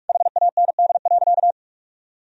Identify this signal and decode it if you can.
Morse code, heard as a keyed tone, about 41 wpm, 700 Hz. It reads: HAND.
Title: Sentences from Top 400 Words and Abbreviations 45wpm